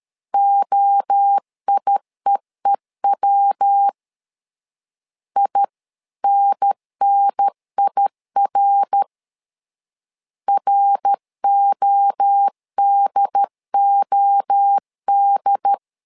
Morse code, heard as a keyed tone, about 13 wpm, 800 Hz. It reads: OIEEW INNIR RODOD
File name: LEZIONI / 3 MWRO / 3 MWRO 12wpm 1730